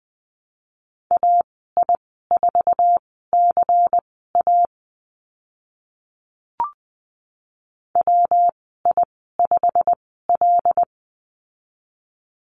Morse code, heard as a keyed tone, about 20 wpm, spaced as for 15 wpm, 700 Hz. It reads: AI4CA WI5L
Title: 154 Lesson 25g - Letter p - Call Signs 20wpm